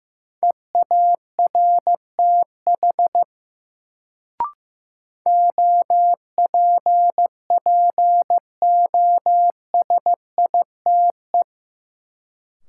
Morse code, heard as a keyed tone, about 16 wpm, 700 Hz. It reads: EARTH OPPOSITE
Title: Sets of 5 Words Encore - New Word Drill - 15wpm